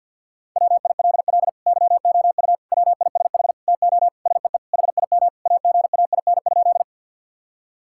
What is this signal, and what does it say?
Morse code, heard as a keyed tone, about 51 wpm, 700 Hz. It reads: WILL YOU WISH TO SEE HIM AGAIN?
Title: Sentences from Top 1600 Words 50wpm